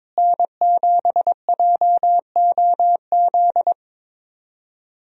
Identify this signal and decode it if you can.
Morse code, heard as a keyed tone, about 22 wpm, 700 Hz. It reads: N7JOZ
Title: USA Call Signs - 5 Characters - Repeated 2x - 22wpm